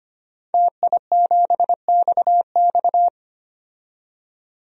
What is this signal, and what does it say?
Morse code, heard as a keyed tone, about 25 wpm, 700 Hz. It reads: TI7XX
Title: DXCC-100 Call Signs - 5 Characters - 25wpm